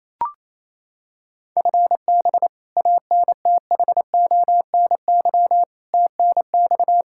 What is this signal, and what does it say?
Morse code, heard as a keyed tone, about 28 wpm, 700 Hz. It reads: FB ANTHONY TNX
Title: Practice QSO - Exchange 3 - More Info - 28wpm